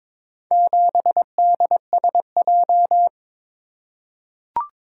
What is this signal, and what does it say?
Morse code, heard as a keyed tone, about 22 wpm, 700 Hz. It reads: 7DSJ
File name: practice / 22wpm / Four Letter-Number Combinations - 22wpm